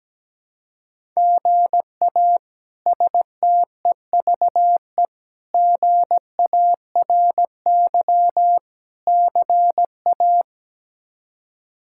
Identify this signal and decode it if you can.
Morse code, heard as a keyed tone, about 17 wpm, 700 Hz. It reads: GA STEVE GARY CA
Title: K1USN SST QSO Practice - 17wpm